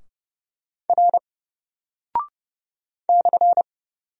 Morse code, heard as a keyed tone, about 30 wpm, 700 Hz. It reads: R /
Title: Single Character (Letter:Number:Punctuation:Prosign) 30wpm